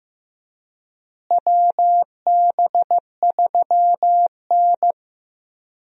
Morse code, heard as a keyed tone, about 16 wpm, 700 Hz. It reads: WB3N
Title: USA Call Signs - 4 Characters - Repeated 2x - 15wpm